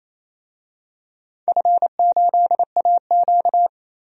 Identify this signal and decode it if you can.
Morse code, heard as a keyed tone, about 27 wpm, 700 Hz. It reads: F8AQ
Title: DXCC-100 Call Signs - 4 Characters - 28wpm